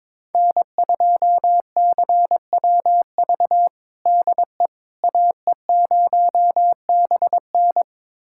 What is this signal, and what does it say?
Morse code, heard as a keyed tone, about 21 wpm, 700 Hz. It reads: N2CWV DE AE0BN
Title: USA Callsign DE Callsign - 5 Characters - 22wpm